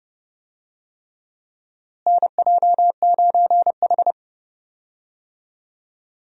Morse code, heard as a keyed tone, about 30 wpm, 700 Hz. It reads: NJ9H
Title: USA Call Signs - 4 Characters - Repeated 3x - 30wpm